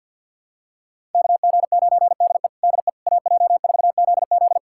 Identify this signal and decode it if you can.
Morse code, heard as a keyed tone, about 47 wpm, 700 Hz. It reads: KG9DE DE AJ4ZZ
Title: USA Callsign DE Callsign - 5 Characters - 50wpm